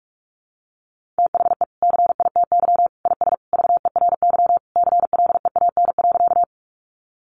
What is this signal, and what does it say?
Morse code, heard as a keyed tone, about 44 wpm, 700 Hz. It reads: THE CITY IS VERY CLEAN.
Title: Easy Phrases 45wpm